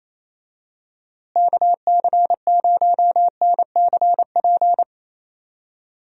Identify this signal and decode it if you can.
Morse code, heard as a keyed tone, about 28 wpm, 700 Hz. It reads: KC0NCP
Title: USA Call Signs - 6 Characters - Repeated 3x - 28wpm